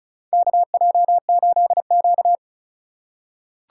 Morse code, heard as a keyed tone, about 36 wpm, 690 Hz. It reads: KJ8Q
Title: USA Call Signs - 4 Characters - 2x Character Spacing - 35wpm